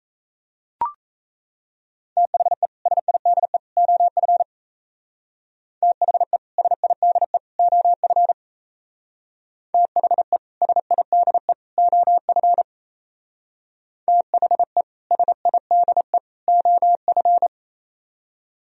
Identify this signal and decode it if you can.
Morse code, heard as a keyed tone, about 34 wpm, 700 Hz. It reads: THE SIDE OF THE SIDE OF THE SIDE OF THE SIDE OF
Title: Speed-Racing - Sets of 3 Words 28wpm